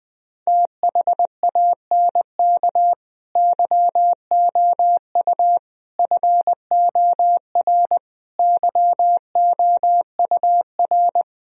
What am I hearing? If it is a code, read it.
Morse code, heard as a keyed tone, about 20 wpm, 700 Hz. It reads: THANK YOU FOR YOUR
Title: Sentences from Top 1000 Words - Review Part 3 - 20wpm